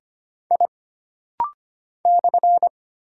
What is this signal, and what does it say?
Morse code, heard as a keyed tone, about 25 wpm, 700 Hz. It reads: I /
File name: Single Character (Letter:Number:Punctuation:Prosign) - Rapid-Fire - 25wpm